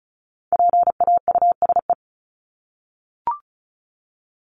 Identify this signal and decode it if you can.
Morse code, heard as a keyed tone, about 34 wpm, 700 Hz. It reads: PAUSE
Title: Top 1501-1750 Words 35wpm